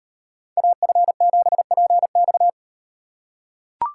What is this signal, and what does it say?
Morse code, heard as a keyed tone, about 38 wpm, 700 Hz. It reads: AF7PX